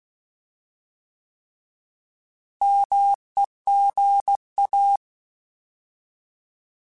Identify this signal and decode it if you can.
Morse code, heard as a keyed tone, about 16 wpm, 800 Hz. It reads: MEGA
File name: Alphabetical / Alphabetical mp3s / Ms @15